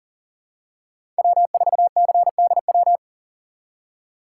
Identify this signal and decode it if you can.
Morse code, heard as a keyed tone, about 37 wpm, 700 Hz. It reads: W4CDW
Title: USA Call Signs - 5 Characters - 3x Character Spacing - 40wpm